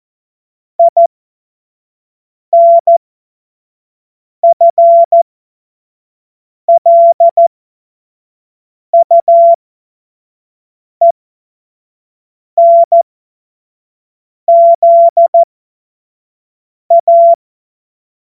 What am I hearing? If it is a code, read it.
Morse code, heard as a keyed tone, about 14 wpm, spaced as for 5 wpm, 680 Hz. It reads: INFLUENZA
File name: Morsekurs / Witze / Witz 06. Juli 2021, 25 F70